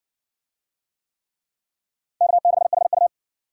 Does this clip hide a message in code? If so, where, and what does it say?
Yes: Morse code from 2.2 to 3.1 s, heard as a keyed tone, about 66 wpm, 700 Hz. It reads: K6HU